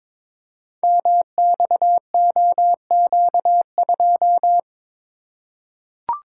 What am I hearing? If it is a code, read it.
Morse code, heard as a keyed tone, about 22 wpm, 700 Hz. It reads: MXOQ2
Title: Code Groups - 5 Letter:Numbers - 22wpm